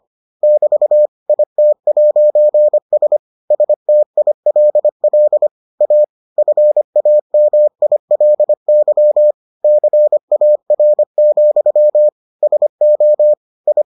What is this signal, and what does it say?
Morse code, heard as a keyed tone, about 25 wpm, 600 Hz. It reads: <BT> IT'S STILL A FAMILY CAR, SO I